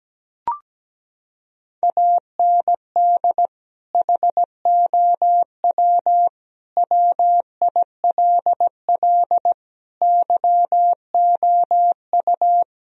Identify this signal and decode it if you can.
Morse code, heard as a keyed tone, about 17 wpm, 700 Hz. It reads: AND HOW WILL YOU
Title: Sentences from Top 1000 Words - Review Part 4 - 17wpm